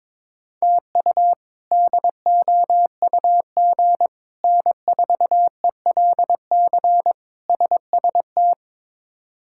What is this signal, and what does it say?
Morse code, heard as a keyed tone, about 22 wpm, 700 Hz. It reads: TU DOUG N4ELC SST
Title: K1USN SST QSO Practice - 22wpm